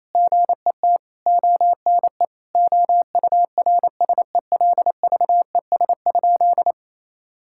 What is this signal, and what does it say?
Morse code, heard as a keyed tone, about 27 wpm, 700 Hz. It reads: GET ONE OURSELVES?
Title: Sentences from Top 1600 Words 28wpm